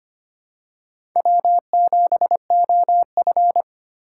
Morse code, heard as a keyed tone, about 25 wpm, 700 Hz. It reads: W7OF
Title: USA Call Signs - 4 Characters - 2x Character Spacing - 25wpm